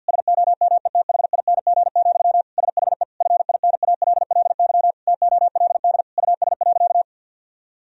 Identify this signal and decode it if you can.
Morse code, heard as a keyed tone, about 52 wpm, 700 Hz. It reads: SOMETHING, SHE FINALLY TOLD US.
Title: Sentences from Top 1000 Words - Review Part 3 - 50wpm